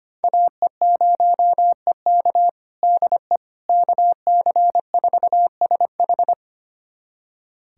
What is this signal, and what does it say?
Morse code, heard as a keyed tone, about 25 wpm, 700 Hz. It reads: AE0EK DE KC4SH
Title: USA Callsign DE Callsign - 5 Characters - 25wpm